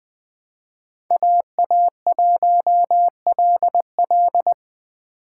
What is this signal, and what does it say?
Morse code, heard as a keyed tone, about 20 wpm, 700 Hz. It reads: AA1LL
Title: USA Call Signs - 5 Characters - 3x Character Spacing - 20wpm